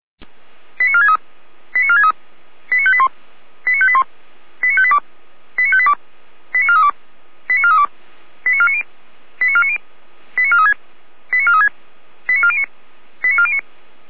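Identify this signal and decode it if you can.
Five-tone selective call (ZVEI), heard as quick runs of five tones, tones of about 70 ms. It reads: 89463 89463 89581 89581 89582 89582 89432 89432 89490 89490 89436 89436 89409 89409
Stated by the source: ZVEI five-tone selective call received on an SDR.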